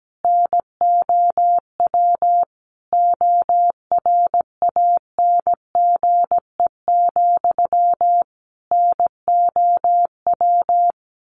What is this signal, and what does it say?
Morse code, heard as a keyed tone, about 17 wpm, 700 Hz. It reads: NOW ORANGE, NOW
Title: A Sound Of Thunder 17wpm